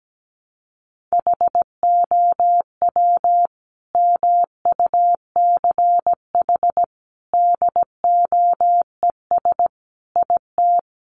Morse code, heard as a keyed tone, about 17 wpm, 700 Hz. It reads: HOW MUCH DOES IT
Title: Sentences from Top 700 Words 17wpm